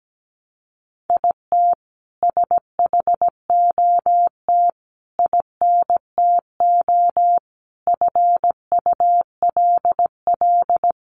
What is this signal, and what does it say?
Morse code, heard as a keyed tone, about 17 wpm, 700 Hz. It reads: IT SHOT INTO FULL
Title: Sentences from Top 1100 Words 17wpm